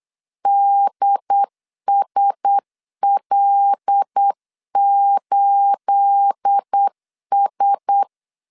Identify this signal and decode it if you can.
Morse code, heard as a keyed tone, about 8 wpm, 800 Hz. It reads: DSL8S
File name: LEZIONI / 10 SIGNS / 10B INT 8wpm 3534